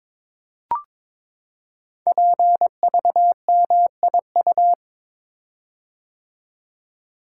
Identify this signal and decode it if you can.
Morse code, heard as a keyed tone, about 22 wpm, 700 Hz. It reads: PVMIU
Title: Code Groups - 5 Letter:Numbers - 22wpm